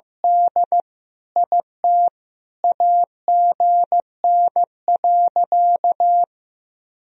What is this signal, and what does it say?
Morse code, heard as a keyed tone, about 16 wpm, 700 Hz. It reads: D IT AGN.
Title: Sentences from Top 300 Words and Abbreviations 15wpm